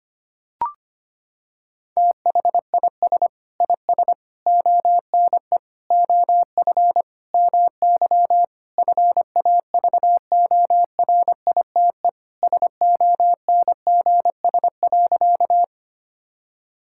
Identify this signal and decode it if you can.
Morse code, heard as a keyed tone, about 25 wpm, 700 Hz. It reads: THIS IS ONE OF MY FAVORITE SONGS.
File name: practice / 25wpm / Sentences with 4-Word Phrases - 25wpm